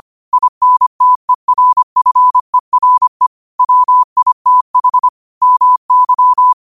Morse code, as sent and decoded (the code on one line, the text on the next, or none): .. -. - . .-. ..-. .-. / .-- .. - .... / -- -.--
INTERFERE WITH MY